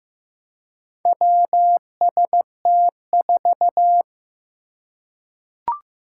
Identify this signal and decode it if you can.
Morse code, heard as a keyed tone, about 15 wpm, 700 Hz. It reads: WST4
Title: Four Challenging Letter-Number Combinations 15wpm